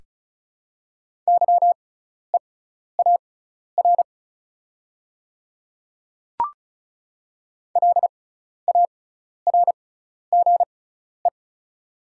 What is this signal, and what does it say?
Morse code, heard as a keyed tone, about 36 wpm, spaced as for 12 wpm, 690 Hz. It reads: YEAR LARGE
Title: 6x Character Spacing - Top 200 Words - 35wpm